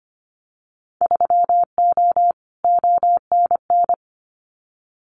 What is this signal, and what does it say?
Morse code, heard as a keyed tone, about 25 wpm, 700 Hz. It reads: 3O ONN
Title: Winter Field Day Exchanges 25wpm